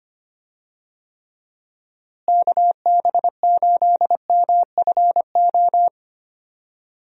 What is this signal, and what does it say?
Morse code, heard as a keyed tone, about 25 wpm, 700 Hz. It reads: KB8MFO
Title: USA Call Signs - 6 Characters - Repeated 2x - 25wpm